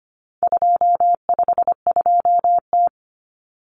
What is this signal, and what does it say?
Morse code, heard as a keyed tone, about 25 wpm, 700 Hz. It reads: 252T
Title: Common Cut Numbers - 4-Digit 25wpm